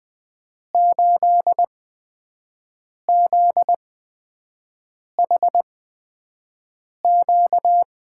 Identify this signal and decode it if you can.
Morse code, heard as a keyed tone, about 20 wpm, spaced as for 5 wpm, 700 Hz. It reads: 8ZHQ
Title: USA Call Signs - 5 Characters - 8x Character Spacing - 20wpm